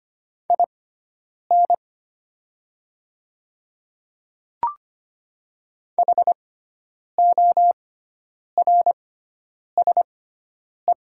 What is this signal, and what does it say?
Morse code, heard as a keyed tone, about 25 wpm, spaced as for 9 wpm, 700 Hz. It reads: IN HORSE
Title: 6x Character Spacing - Top 300 Words - 25wpm